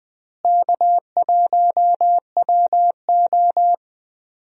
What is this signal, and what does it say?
Morse code, heard as a keyed tone, about 20 wpm, 700 Hz. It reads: K1WO
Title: USA Call Signs - 4 Characters - Repeated 2x - 20wpm